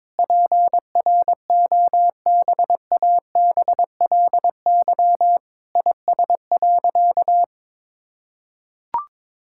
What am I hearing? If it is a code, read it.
Morse code, heard as a keyed tone, about 22 wpm, 700 Hz. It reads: PROBABLY IS.
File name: Sentences with 9-Word Phrases - 22wpm